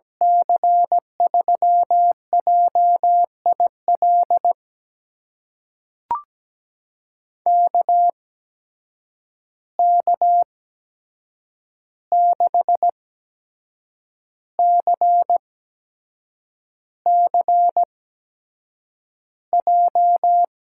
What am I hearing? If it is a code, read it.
Morse code, heard as a keyed tone, about 17 wpm, spaced as for 7 wpm, 700 Hz. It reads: C3JIL KK6CCJ
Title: USA Call Signs - 6 Characters - 8x Character Spacing - 17wpm